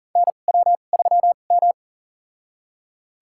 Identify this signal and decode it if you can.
Morse code, heard as a keyed tone, about 40 wpm, spaced as for 29 wpm, 700 Hz. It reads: NW3M
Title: USA Call Signs - 4 Characters - 2x Character Spacing - 40wpm